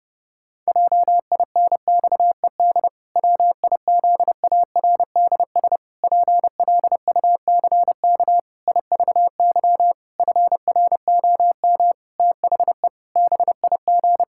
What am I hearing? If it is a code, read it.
Morse code, heard as a keyed tone, about 32 wpm, 700 Hz. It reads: JINXED WIZARDS PLUCK IVY FROM THE BIG
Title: Pangram Challenge v2 - 30wpm